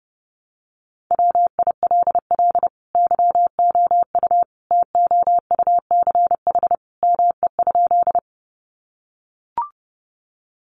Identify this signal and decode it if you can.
Morse code, heard as a keyed tone, about 30 wpm, 700 Hz. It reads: WILL YOU TOUCH ME?
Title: Sentences from Top 1200 Words 30wpm